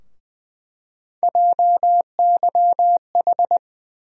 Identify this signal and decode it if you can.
Morse code, heard as a keyed tone, about 20 wpm, 700 Hz. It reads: JYH
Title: USA Call Sign Postfixes - 3 Letters 20wpm